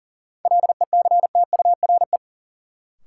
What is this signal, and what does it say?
Morse code, heard as a keyed tone, about 40 wpm, 700 Hz. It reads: LECTURE
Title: Sentences from Top 1600 Words - New Word Drill Alt - 40wpm